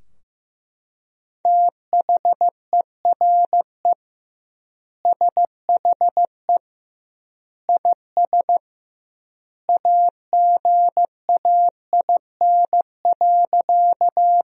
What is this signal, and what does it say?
Morse code, heard as a keyed tone, about 15 wpm, 700 Hz. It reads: THERE SHE IS AGAIN.
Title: 2x Word-Spacing - Sentences from Top 200 Words - 15wpm